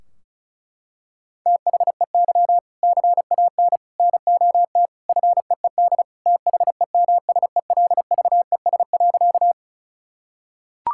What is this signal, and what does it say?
Morse code, heard as a keyed tone, about 36 wpm, 700 Hz. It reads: THEY CAN NOT FEED THEMSELVES.
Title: Sentences from Top 1600 Words 35wpm